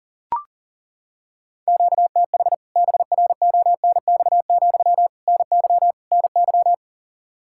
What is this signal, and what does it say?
Morse code, heard as a keyed tone, about 40 wpm, 700 Hz. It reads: QTH BRONX, NY NY
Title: K1USN SST QSO Practice 40wpm